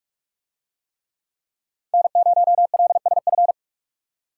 Morse code, heard as a keyed tone, about 44 wpm, 700 Hz. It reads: N0LSF